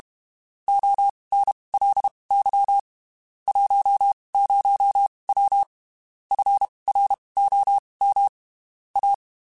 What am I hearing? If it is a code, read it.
Morse code, heard as a keyed tone, about 32 wpm, spaced as for 23 wpm, 800 Hz. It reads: ONLY 10W FROM A
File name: (spaced) / QSOs Spaced-mp3s / WA7QOX 2 @30 sp